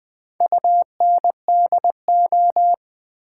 Morse code, heard as a keyed tone, about 20 wpm, 700 Hz. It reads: UNDO
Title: Build A Word - 4 Letters At A Time - 11-Letter Words - 20wpm